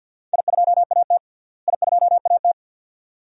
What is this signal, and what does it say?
Morse code, heard as a keyed tone, about 50 wpm, 700 Hz. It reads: I2AT I2AT
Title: DXCC-100 Call Signs - 4 Characters - Repeated 2x - 50wpm